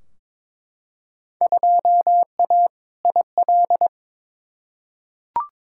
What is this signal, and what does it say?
Morse code, heard as a keyed tone, about 22 wpm, 700 Hz. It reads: 2A IL
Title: Field Day Exchanges Encore - 22wpm